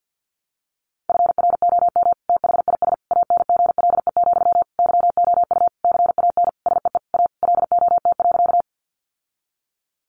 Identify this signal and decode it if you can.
Morse code, heard as a keyed tone, about 50 wpm, 700 Hz. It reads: FROM THIS ANGLE, YOU CAN SEE A LOT.
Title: Sentences from Top 1500 Words 50wpm